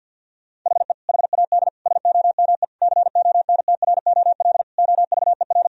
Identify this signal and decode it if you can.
Morse code, heard as a keyed tone, about 50 wpm, 700 Hz. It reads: HE HAD SOME CONTROL OVER
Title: Sentences from Top 1000 Words - Review Part 3 - 50wpm